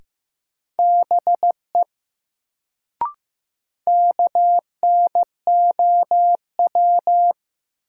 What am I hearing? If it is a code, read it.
Morse code, heard as a keyed tone, about 15 wpm, 700 Hz. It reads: BE KNOW
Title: Sets of 2 Words - New Word Drill - 15wpm